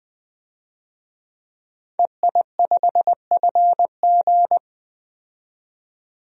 Morse code, heard as a keyed tone, about 20 wpm, 700 Hz. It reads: EI5FG
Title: DXCC-100 Call Signs - 5 Characters - 20wpm